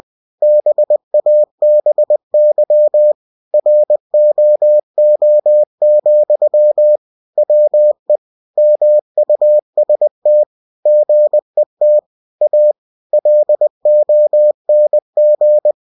Morse code, heard as a keyed tone, about 20 wpm, 600 Hz. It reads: BABY ROO, WE MUST GET A LONG
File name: cw-books / winnie-the-pooh / winnie-20wpm-page-0063